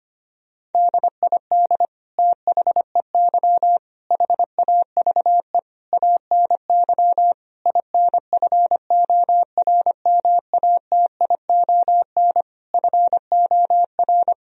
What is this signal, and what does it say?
Morse code, heard as a keyed tone, about 25 wpm, 700 Hz. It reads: DID THEY HAVE ANY INFORMATION FOR
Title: Sentences from Top 1000 Words - Review Part 3 - 25wpm